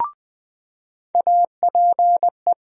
Morse code, heard as a keyed tone, about 20 wpm, 700 Hz. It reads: APE
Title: USA Call Sign Postfixes - 3 Letters 20wpm